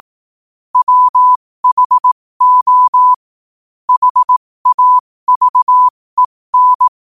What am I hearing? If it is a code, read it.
Morse code, heard as a keyed tone, about 18 wpm, spaced as for 15 wpm, 1000 Hz. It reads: WHO HAVEN
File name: qotdfiles / 15wpm